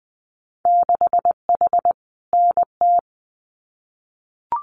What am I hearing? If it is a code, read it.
Morse code, heard as a keyed tone, about 20 wpm, 700 Hz. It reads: 6H NT
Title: Winter Field Day Exchanges Encore 20wpm